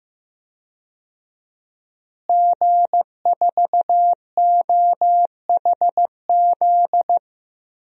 Morse code, heard as a keyed tone, about 15 wpm, 700 Hz. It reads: G4OHZ
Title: DXCC-100 Call Signs - 5 Characters - Repeated 3x - 15wpm